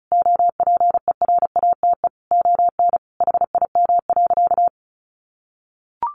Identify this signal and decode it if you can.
Morse code, heard as a keyed tone, about 34 wpm, 700 Hz. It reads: OPERATE ON HIM.
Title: Sentences from Top 1500 Words 35wpm